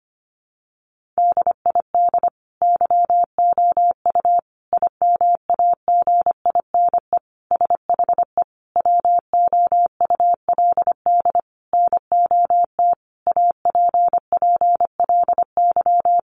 Morse code, heard as a keyed tone, about 25 wpm, 700 Hz. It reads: DID YOU IMAGINE SHE WOULD NOT APPLY
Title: Sentences from Top 1400 Words 25wpm